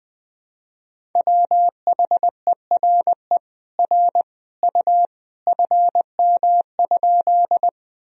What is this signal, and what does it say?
Morse code, heard as a keyed tone, about 21 wpm, 700 Hz. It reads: WHERE R U FM?
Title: Sentences from Top 200 Words and Abbreviations 20wpm